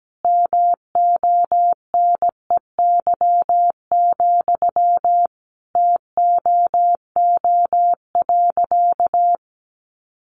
Morse code, heard as a keyed tone, about 17 wpm, 700 Hz. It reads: MONEY, TOO.
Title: Sentences from Top 900 Words 17wpm